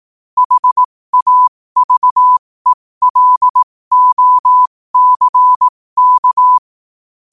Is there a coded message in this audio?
Yes: Morse code from 0.4 to 6.6 s, heard as a keyed tone, about 18 wpm, spaced as for 15 wpm, 1000 Hz. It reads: HAVELOCK